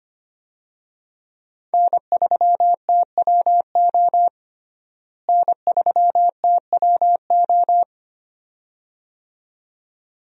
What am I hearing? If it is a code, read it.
Morse code, heard as a keyed tone, about 25 wpm, 700 Hz. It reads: N3TWO N3TWO
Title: USA Call Signs - 5 Characters - Repeated 2x - 25wpm